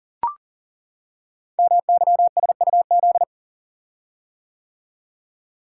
Morse code, heard as a keyed tone, about 40 wpm, 700 Hz. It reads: MYSUZ
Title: Code Groups - 5 Letters - 40wpm